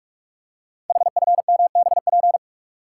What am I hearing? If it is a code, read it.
Morse code, heard as a keyed tone, about 44 wpm, 700 Hz. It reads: HFMBP
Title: Code Groups - 5 Letter:Numbers - 45wpm